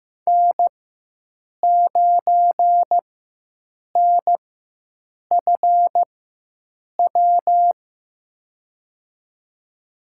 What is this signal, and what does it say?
Morse code, heard as a keyed tone, about 15 wpm, spaced as for 7 wpm, 700 Hz. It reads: N9NFW
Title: USA Call Signs - 5 Characters - 4x Character Spacing - 15wpm